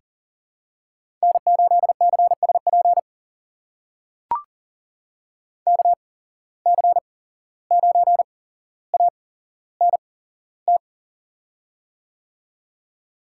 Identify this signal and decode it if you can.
Morse code, heard as a keyed tone, about 41 wpm, spaced as for 16 wpm, 700 Hz. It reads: N8CSP KC9ANT
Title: USA Call Signs - 8x Character Spacing - 40wpm